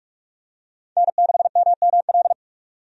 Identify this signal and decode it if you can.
Morse code, heard as a keyed tone, about 42 wpm, 700 Hz. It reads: N6MML
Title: USA Call Signs - 5 Characters - 45wpm